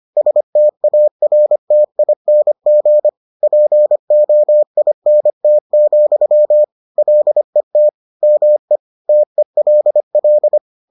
Morse code, heard as a keyed tone, about 25 wpm, 600 Hz. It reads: STARTING POINT, LET ME TELL